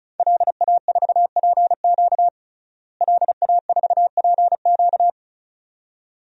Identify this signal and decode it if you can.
Morse code, heard as a keyed tone, about 34 wpm, 700 Hz. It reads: LA4PQ LA4PQ
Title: DXCC-100 Call Signs - 5 Characters - Repeated 2x - 35wpm